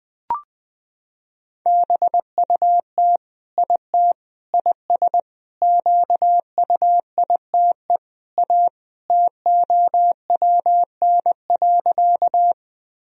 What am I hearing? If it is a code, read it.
Morse code, heard as a keyed tone, about 19 wpm, 700 Hz. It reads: BUT IT IS QUITE A TOWN.
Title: Sentences from Top 1000 Words - Review Part 2 - 20wpm